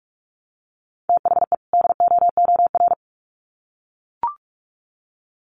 Morse code, heard as a keyed tone, about 44 wpm, 700 Hz. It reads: THE DOOR
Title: Sets of 2 Words 45wpm